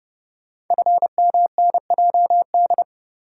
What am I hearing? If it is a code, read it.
Morse code, heard as a keyed tone, about 29 wpm, 700 Hz. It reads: FMNJD
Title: Code Groups - 5 Letters - 30wpm